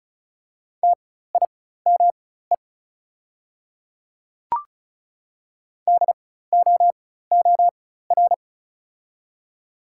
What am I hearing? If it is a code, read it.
Morse code, heard as a keyed tone, about 36 wpm, spaced as for 16 wpm, 700 Hz. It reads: TIME DOOR
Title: 4x Character Spacing - Top 300 Words - 35wpm